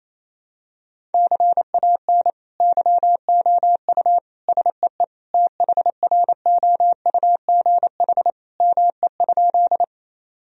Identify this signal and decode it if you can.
Morse code, heard as a keyed tone, about 28 wpm, 700 Hz. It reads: CAN YOU SEE THROUGH ME?
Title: Sentences from Top 1000 Words - Review Part 1 - 28wpm